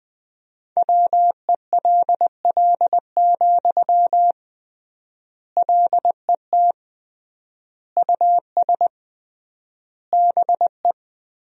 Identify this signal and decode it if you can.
Morse code, heard as a keyed tone, about 21 wpm, 700 Hz. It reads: WELL, LET US BE
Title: 3x Word-Spacing - Sentences from Top 300 Words - 20wpm